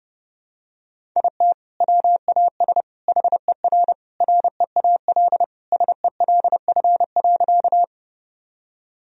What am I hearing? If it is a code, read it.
Morse code, heard as a keyed tone, about 30 wpm, 700 Hz. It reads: IT WAS HER REAL SELF.
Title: Sentences from Top 1000 Words - Review Part 1 - 30wpm